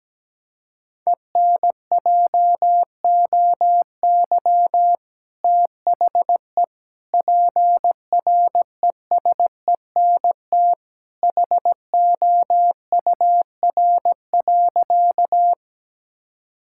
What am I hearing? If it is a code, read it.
Morse code, heard as a keyed tone, about 17 wpm, 700 Hz. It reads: ENJOY THE PRESENT HOUR.
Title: Sentences from Top 1000 Words - Review Part 5 - 17wpm